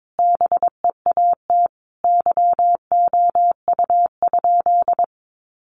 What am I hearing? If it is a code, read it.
Morse code, heard as a keyed tone, about 22 wpm, 700 Hz. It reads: BEAT YOU?
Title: Sentences from Top 1300 Words 22wpm